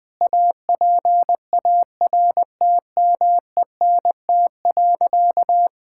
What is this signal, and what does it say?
Morse code, heard as a keyed tone, about 21 wpm, 700 Hz. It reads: APARTMENT.
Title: Sentences from Top 1000 Words - Review Part 6 - 20wpm